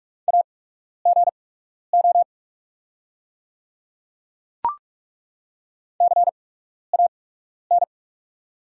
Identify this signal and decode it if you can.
Morse code, heard as a keyed tone, about 48 wpm, spaced as for 12 wpm, 690 Hz. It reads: AGO CAN
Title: 8x Character Spacing - Top 500 Words - 45wpm